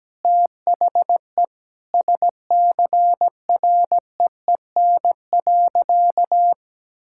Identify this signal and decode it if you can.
Morse code, heard as a keyed tone, about 17 wpm, 700 Hz. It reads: THE SCREEN.
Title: Sentences from Top 1000 Words - Review Part 6 - 17wpm